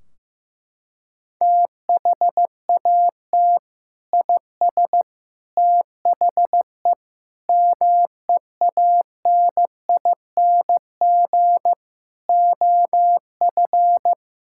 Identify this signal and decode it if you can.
Morse code, heard as a keyed tone, about 15 wpm, 700 Hz. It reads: THAT IS THE MEANING OF
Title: Sentences from Top 1600 Words 15wpm